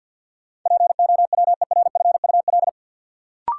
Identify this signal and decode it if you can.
Morse code, heard as a keyed tone, about 52 wpm, 700 Hz. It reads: POWERFUL
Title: Sentences from Top 1300 Words - New Word Drill Alt - 50wpm